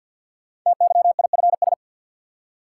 Morse code, heard as a keyed tone, about 48 wpm, 700 Hz. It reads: TYIFS
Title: Code Groups - 5 Letter:Numbers - 50wpm